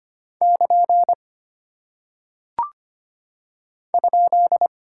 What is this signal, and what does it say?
Morse code, heard as a keyed tone, about 25 wpm, 700 Hz. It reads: <KN> ?
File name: Single Punctuation-Prosign - 25wpm